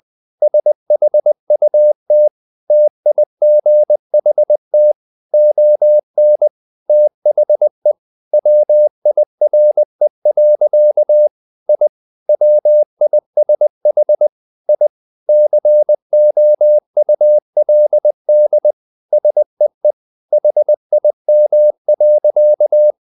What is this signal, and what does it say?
Morse code, heard as a keyed tone, about 20 wpm, 600 Hz. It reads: SHUT TIGHT ON THE WIRE. I WISH I COULD SEE HIM.
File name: the old man and the sea / the old man and the sea part 1 20wpm